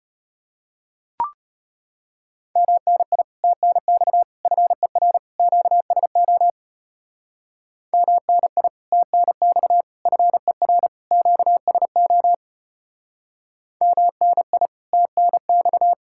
Morse code, heard as a keyed tone, about 31 wpm, 700 Hz. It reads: MNI TNX FER QSO MNI TNX FER QSO MNI TNX
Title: Speed-Racing Common QSO Phrases - 25wpm